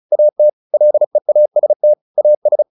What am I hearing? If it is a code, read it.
Morse code, heard as a keyed tone, about 34 wpm, 600 Hz. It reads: AT LEAST AS